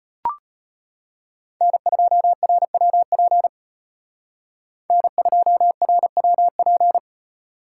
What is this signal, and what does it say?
Morse code, heard as a keyed tone, about 30 wpm, 700 Hz. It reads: N2RWP N2RWP
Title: Speed Racing - USA 1x3 Call Signs 25wpm